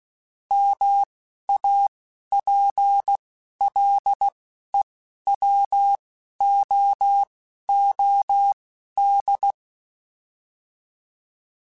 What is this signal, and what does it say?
Morse code, heard as a keyed tone, about 16 wpm, spaced as for 11 wpm, 800 Hz. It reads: MAPLEWOOD
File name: fromSMs / FromSMs mp3s / Cities@10